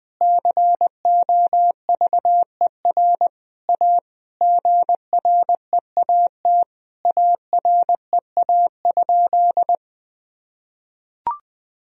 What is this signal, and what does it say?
Morse code, heard as a keyed tone, about 20 wpm, 700 Hz. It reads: COVER A GREAT AREA?
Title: Sentences from Top 1000 Words - Review Part 2 - 20wpm